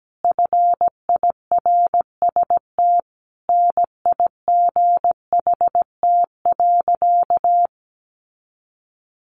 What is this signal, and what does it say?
Morse code, heard as a keyed tone, about 17 wpm, 700 Hz. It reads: FIRST NIGHT.
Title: Sentences from Top 300 Words 17wpm